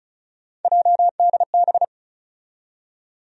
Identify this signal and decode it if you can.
Morse code, heard as a keyed tone, about 36 wpm, 700 Hz. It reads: JDB
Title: USA Call Sign Postfixes - 3 Letters 35wpm